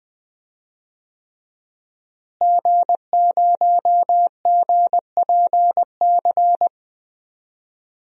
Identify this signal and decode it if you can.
Morse code, heard as a keyed tone, about 21 wpm, 700 Hz. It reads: G0GPC
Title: DXCC-100 Call Signs - 5 Characters - Repeated 3x - 20wpm